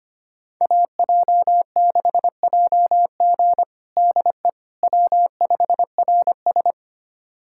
Morse code, heard as a keyed tone, about 25 wpm, 700 Hz. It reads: AJ6JG DE W5RS
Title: USA Callsign DE Callsign 25wpm